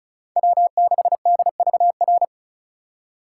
Morse code, heard as a keyed tone, about 36 wpm, 700 Hz. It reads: W6DVR
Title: USA Call Signs - 5 Characters - 3x Character Spacing - 35wpm